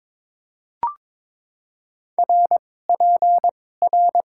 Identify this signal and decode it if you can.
Morse code, heard as a keyed tone, about 22 wpm, spaced as for 16 wpm, 700 Hz. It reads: RPR
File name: Top 90 QSO Elements - 2x Character Spacing - 22wpm